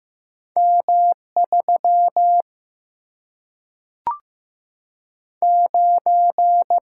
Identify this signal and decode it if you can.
Morse code, heard as a keyed tone, about 15 wpm, 700 Hz. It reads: M3 9
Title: Single-Letter Single-Number 15wpm